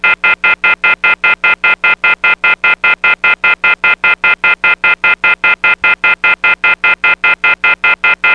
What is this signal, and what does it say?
Off-hook warning tone, 1400 + 2060 + 2450 + 2600 Hz, with noise.
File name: PhoneSoundz 1.0.rsrc / snd 24110 Off Hook.wav / snd 24110 Off Hook